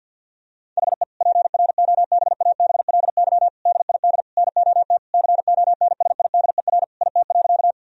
Morse code, heard as a keyed tone, about 50 wpm, 700 Hz. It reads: HE PROBABLY DID NOT CONSIDER IT.
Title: Sentences from Top 1000 Words - Review Part 4 - 50wpm